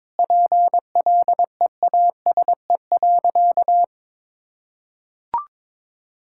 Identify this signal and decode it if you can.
Morse code, heard as a keyed tone, about 22 wpm, 700 Hz. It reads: PLEASE.
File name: practice / 22wpm / Sentences with 4-Word Phrases - 22wpm